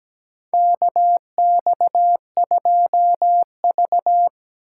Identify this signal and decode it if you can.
Morse code, heard as a keyed tone, about 17 wpm, 700 Hz. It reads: KX2V
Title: USA Call Signs - 3x Character Spacing - 17wpm